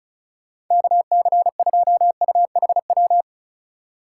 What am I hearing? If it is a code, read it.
Morse code, heard as a keyed tone, about 36 wpm, 700 Hz. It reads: KC2UHW
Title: USA Call Signs - 6 Characters - 4x Character Spacing - 35wpm